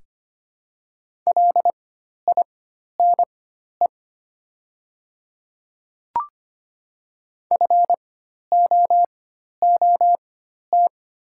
Morse code, heard as a keyed tone, about 25 wpm, spaced as for 12 wpm, 700 Hz. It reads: LINE FOOT
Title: 4x Character Spacing - Top 500 Words - 25wpm